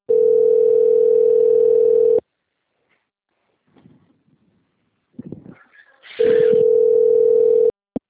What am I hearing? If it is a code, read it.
Ringback tone (North America), 440 + 480 Hz.
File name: out- - - - - .10289